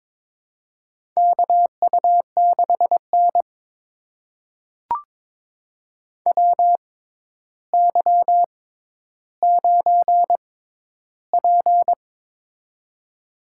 Morse code, heard as a keyed tone, about 22 wpm, spaced as for 11 wpm, 700 Hz. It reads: KU6N WY9P
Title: USA Call Signs - 4 Characters - 6x Character Spacing - 22wpm